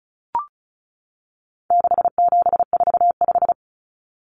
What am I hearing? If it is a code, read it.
Morse code, heard as a keyed tone, about 34 wpm, 700 Hz. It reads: 6745
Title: Common Cut Numbers - 4-Digit 35wpm